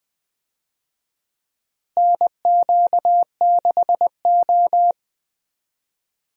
Morse code, heard as a keyed tone, about 20 wpm, 700 Hz. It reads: NQ6O
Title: USA Call Signs - 4 Characters - Repeated 3x - 20wpm